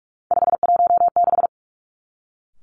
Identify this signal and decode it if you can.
Morse code, heard as a keyed tone, about 46 wpm, 700 Hz. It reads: 516
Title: Three Challenging Letter-Number Combinations 45wpm